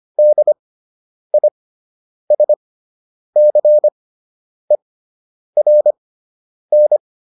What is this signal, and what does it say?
Morse code, heard as a keyed tone, about 25 wpm, spaced as for 9 wpm, 600 Hz. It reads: DISCERN